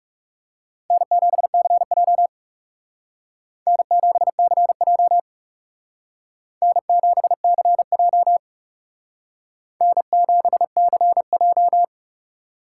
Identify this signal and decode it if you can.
Morse code, heard as a keyed tone, about 39 wpm, 700 Hz. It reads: N7CJ N7CJ N7CJ N7CJ